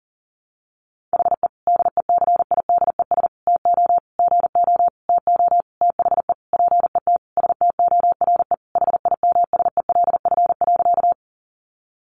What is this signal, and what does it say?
Morse code, heard as a keyed tone, about 40 wpm, 700 Hz. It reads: HE DECIDES TO GO TO THE PET STORE HIMSELF.
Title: Additional Easy Phrases 40wpm